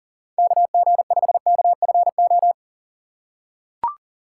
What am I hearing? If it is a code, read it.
Morse code, heard as a keyed tone, about 39 wpm, 700 Hz. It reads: KG5KFO
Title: USA Call Signs - 6 Characters - 3x Character Spacing - 40wpm